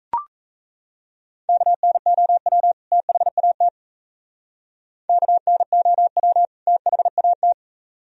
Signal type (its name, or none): Morse code